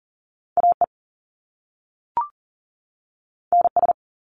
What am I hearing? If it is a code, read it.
Morse code, heard as a keyed tone, about 40 wpm, 700 Hz. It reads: AE NS
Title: Top Two Letter Combinations in Words 40wpm